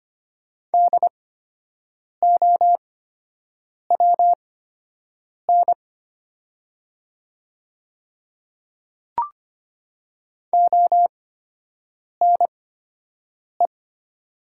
Morse code, heard as a keyed tone, about 25 wpm, spaced as for 7 wpm, 700 Hz. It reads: DOWN ONE
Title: 8x Character Spacing - Top 100 Words - 25wpm